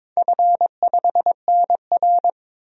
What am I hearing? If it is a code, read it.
Morse code, heard as a keyed tone, about 22 wpm, 700 Hz. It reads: F5NR
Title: DXCC-100 Call Signs - 4 Characters - 22wpm